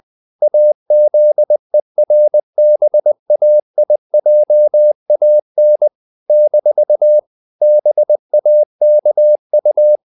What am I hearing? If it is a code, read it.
Morse code, heard as a keyed tone, about 20 wpm, 600 Hz. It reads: AZERBAIJAN - BAKU
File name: CW / Practice / NationalCapitals20